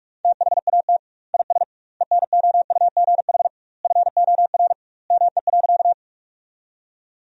Morse code, heard as a keyed tone, about 41 wpm, 700 Hz. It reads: THAT IS ENOUGH FOR ME.
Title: 2x Word-Spacing - Sentences from Top 400 Words - 45wpm